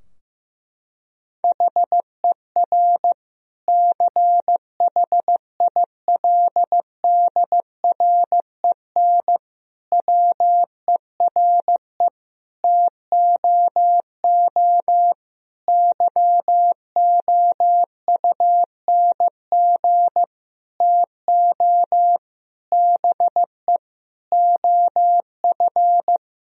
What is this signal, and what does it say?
Morse code, heard as a keyed tone, about 15 wpm, 700 Hz. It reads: HER CHILDREN WERE TOO YOUNG TO BE OF
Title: Sentences from Top 1000 Words - Review Part 4 - 15wpm